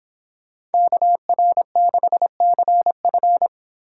Morse code, heard as a keyed tone, about 31 wpm, 700 Hz. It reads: KR6CF